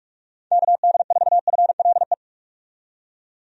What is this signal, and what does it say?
Morse code, heard as a keyed tone, about 46 wpm, 700 Hz. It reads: KD4FLE
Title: USA Call Signs - 6 Characters - 45wpm